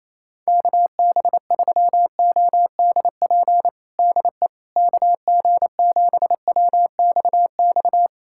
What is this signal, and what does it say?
Morse code, heard as a keyed tone, about 28 wpm, 700 Hz. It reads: KB3ODP DE KG7WXX
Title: USA Callsign DE Callsign 28wpm